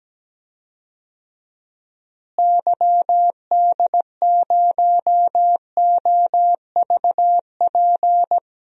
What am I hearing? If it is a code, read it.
Morse code, heard as a keyed tone, about 17 wpm, 700 Hz. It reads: YD0OVP
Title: DXCC-100 Call Signs - 6 Characters - Repeated 3x - 17wpm